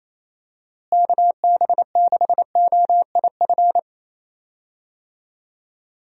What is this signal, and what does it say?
Morse code, heard as a keyed tone, about 28 wpm, 700 Hz. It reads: KB6OIF
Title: USA Call Signs - 6 Characters - Repeated 2x - 28wpm